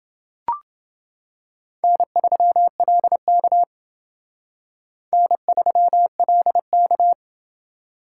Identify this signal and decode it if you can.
Morse code, heard as a keyed tone, about 28 wpm, 700 Hz. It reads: N3LK N3LK